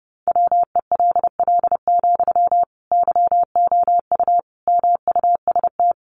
Morse code, heard as a keyed tone, about 30 wpm, 700 Hz. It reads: WELL, YOU MUST
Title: Sentences from Top 1500 Words 30wpm